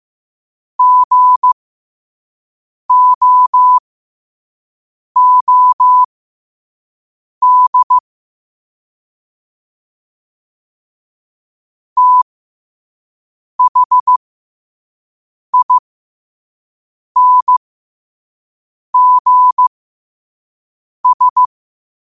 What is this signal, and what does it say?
Morse code, heard as a keyed tone, about 15 wpm, spaced as for 5 wpm, 1000 Hz. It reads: GOOD THINGS
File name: qotdfiles / 5wpm